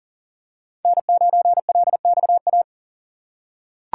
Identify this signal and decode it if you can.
Morse code, heard as a keyed tone, about 40 wpm, 700 Hz. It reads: N9LXA
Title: USA Call Signs - 5 Characters - 2x Character Spacing - 40wpm